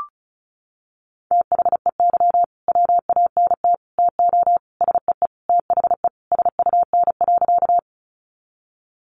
Morse code, heard as a keyed tone, about 34 wpm, 700 Hz. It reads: THEY WANT TO SEE THE SUN.
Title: Easy Phrases 35wpm